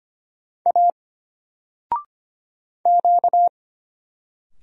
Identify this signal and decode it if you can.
Morse code, heard as a keyed tone, about 25 wpm, 700 Hz. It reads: A Q
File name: Single-Letter-Number-25wpm